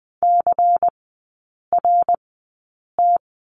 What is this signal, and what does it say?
Morse code, heard as a keyed tone, about 20 wpm, spaced as for 8 wpm, 690 Hz. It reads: CRT
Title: 172 Lesson 28e - Number 9 - Three Characters 20wpm